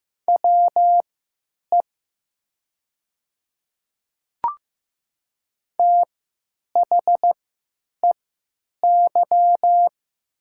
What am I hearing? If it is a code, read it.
Morse code, heard as a keyed tone, about 15 wpm, spaced as for 9 wpm, 690 Hz. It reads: WE THEY
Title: 3x Character Spacing - Top 100 Words - 15wpm